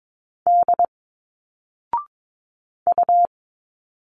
Morse code, heard as a keyed tone, about 22 wpm, 700 Hz. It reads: D U
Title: Single Letters 22wpm